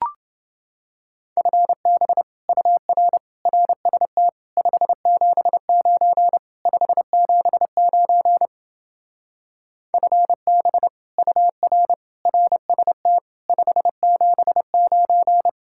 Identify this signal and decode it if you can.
Morse code, heard as a keyed tone, about 31 wpm, 700 Hz. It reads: FB UR RST 579 579 FB UR RST 579